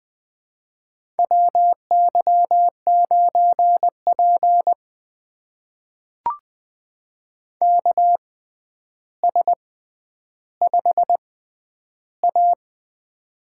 Morse code, heard as a keyed tone, about 20 wpm, spaced as for 10 wpm, 700 Hz. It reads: WY9P KS5A
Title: USA Call Signs - 4 Characters - 6x Character Spacing - 20wpm